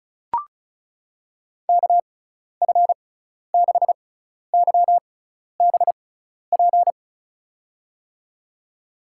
Morse code, heard as a keyed tone, about 34 wpm, spaced as for 12 wpm, 700 Hz. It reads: KF6YBP
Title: USA Call Signs - 6 Characters - 6x Character Spacing - 35wpm